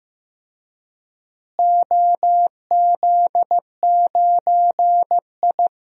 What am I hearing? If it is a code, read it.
Morse code, heard as a keyed tone, about 15 wpm, 700 Hz. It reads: OZ9I
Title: DXCC-100 Call Signs - 4 Characters - Repeated 3x - 15wpm